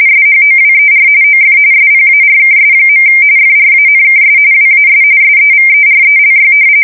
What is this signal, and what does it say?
RTTY, about 45.45 baud. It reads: SHOWER OF SPARKS. FRANKS WAS SEVERELY
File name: fsk170 / battleshipnj